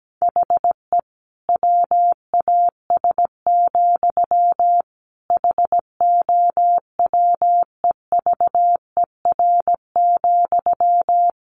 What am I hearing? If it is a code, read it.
Morse code, heard as a keyed tone, about 17 wpm, 700 Hz. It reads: HE WAS, HOWEVER,
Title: Sentences from Top 1100 Words 17wpm